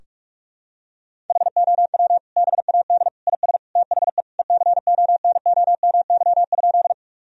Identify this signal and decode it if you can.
Morse code, heard as a keyed tone, about 41 wpm, 700 Hz. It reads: HOW BAD IS THE ECONOMY?
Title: Sentences from Top 1000 Words - Review Part 3 - 45wpm